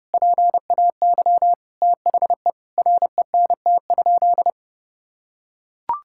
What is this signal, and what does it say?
Morse code, heard as a keyed tone, about 32 wpm, 700 Hz. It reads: PAY THE RENT?
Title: Sentences from Top 1000 Words - Review Part 5 - 30wpm